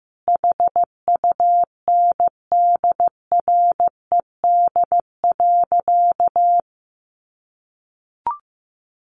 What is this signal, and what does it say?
Morse code, heard as a keyed tone, about 15 wpm, 700 Hz. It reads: HUNDRED.
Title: Sentences from Top 1300 Words 15wpm